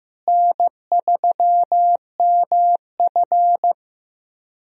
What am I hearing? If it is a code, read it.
Morse code, heard as a keyed tone, about 15 wpm, 700 Hz. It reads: N3MF
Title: USA Call Signs - 4 Characters - Repeated 3x - 15wpm